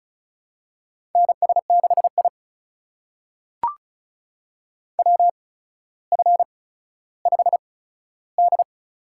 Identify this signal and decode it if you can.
Morse code, heard as a keyed tone, about 36 wpm, spaced as for 15 wpm, 700 Hz. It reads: NS6I WF5D
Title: USA Call Signs - 4 Characters - 8x Character Spacing - 35wpm